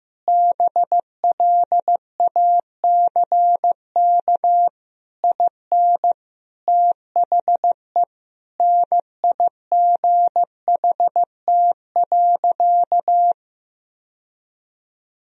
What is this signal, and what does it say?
Morse code, heard as a keyed tone, about 16 wpm, 700 Hz. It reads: BLACK IN THE NIGHT.
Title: Sentences from Top 1000 Words - Review Part 4 - 15wpm